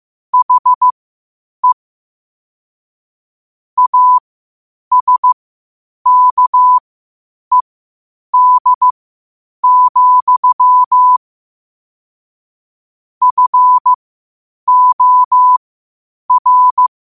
Morse code, heard as a keyed tone, about 15 wpm, spaced as for 8 wpm, 1000 Hz. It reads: HE ASKED, FOR
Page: Quotes for Sat, 16 Aug 2025 in Morse Code at 8 words per minute.